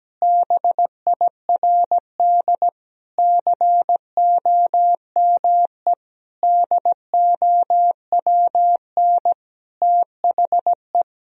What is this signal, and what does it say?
Morse code, heard as a keyed tone, about 17 wpm, 700 Hz. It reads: BIRD COME DOWN THE
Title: Sentences from Top 1000 Words - Review Part 2 - 17wpm